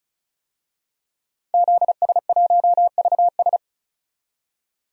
Morse code, heard as a keyed tone, about 34 wpm, 700 Hz. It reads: ZS1VS
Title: DXCC-100 Call Signs - 5 Characters - 35wpm